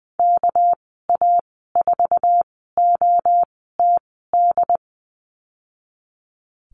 Morse code, heard as a keyed tone, about 20 wpm, spaced as for 15 wpm, 700 Hz. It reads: KA4OTD
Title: 207 Lesson 33g - Letter j - Call Signs 20wpm